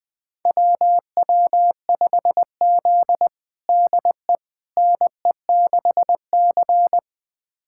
Morse code, heard as a keyed tone, about 20 wpm, 700 Hz. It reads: WW5Z DE NE6C
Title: USA Callsign DE Callsign - 4 Characters - 20wpm